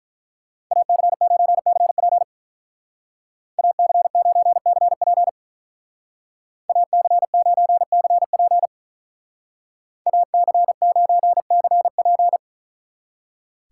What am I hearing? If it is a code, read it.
Morse code, heard as a keyed tone, about 43 wpm, 700 Hz. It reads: AC9CP AC9CP AC9CP AC9CP
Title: Speed Racing - USA Call Signs 35wpm